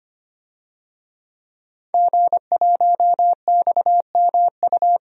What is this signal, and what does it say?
Morse code, heard as a keyed tone, about 25 wpm, 700 Hz. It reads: G1XMU
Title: DXCC-100 Call Signs - 5 Characters - 25wpm